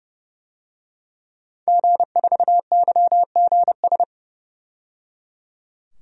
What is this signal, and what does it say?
Morse code, heard as a keyed tone, about 30 wpm, 700 Hz. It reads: G4YGS
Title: DXCC-100 Call Signs - 5 Characters - 30wpm